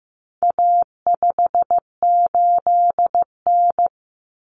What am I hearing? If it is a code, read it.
Morse code, heard as a keyed tone, about 15 wpm, 700 Hz. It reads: A58N
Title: Common Cut Numbers - 4-Digit 15wpm